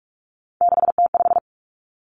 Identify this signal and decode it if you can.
Morse code, heard as a keyed tone, about 46 wpm, 700 Hz. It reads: 6T5
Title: Three Challenging Letter-Number Combinations 45wpm